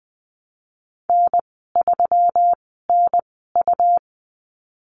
Morse code, heard as a keyed tone, about 20 wpm, spaced as for 15 wpm, 700 Hz. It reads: N3NU
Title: 147 Lesson 24g - Letter y - Call Signs 20wpm